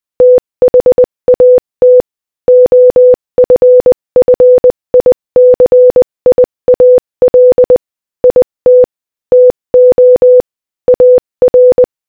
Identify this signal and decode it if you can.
Morse code, heard as a keyed tone, about 20 wpm, 500 Hz. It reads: THAT OFFICIAL IT TO AR
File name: Morse-Code-Audio-Speed-20-13